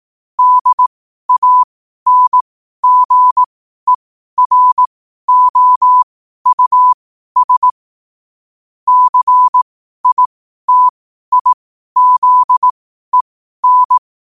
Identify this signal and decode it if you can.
Morse code, heard as a keyed tone, about 18 wpm, spaced as for 12 wpm, 1000 Hz. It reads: DANGEROUS CITIZEN
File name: qotdfiles / 12wpm